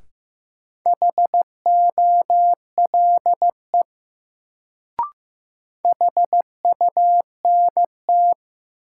Morse code, heard as a keyed tone, about 15 wpm, 700 Hz. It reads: HOLE HUNT
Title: Sentences from Top 1600 Words - New Word Drill Alt - 15wpm